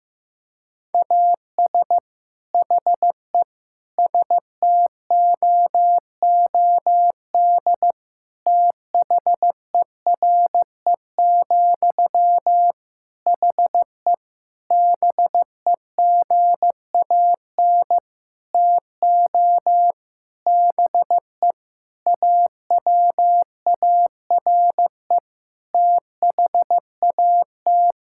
Morse code, heard as a keyed tone, about 15 wpm, 700 Hz. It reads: AS HE STOOD THERE, HE BEGAN TO BE AWARE THAT
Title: Sentences from Top 1600 Words 15wpm